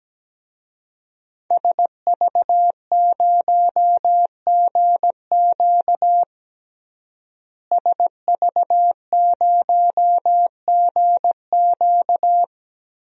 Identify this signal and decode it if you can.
Morse code, heard as a keyed tone, about 17 wpm, 700 Hz. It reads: SV0GQ SV0GQ
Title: DXCC-100 Call Signs - 5 Characters - Repeated 3x - 17wpm